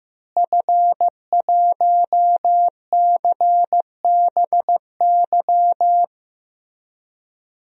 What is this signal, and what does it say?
Morse code, heard as a keyed tone, about 16 wpm, 700 Hz. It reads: F1CBY
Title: DXCC-100 Call Signs - 5 Characters - Repeated 3x - 15wpm